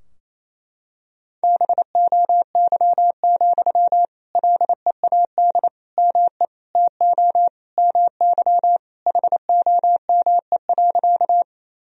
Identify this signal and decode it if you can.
Morse code, heard as a keyed tone, about 27 wpm, 700 Hz. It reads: BOY, LEAD ME TO MY HOME.
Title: Sentences from Top 1000 Words - Review Part 2 - 28wpm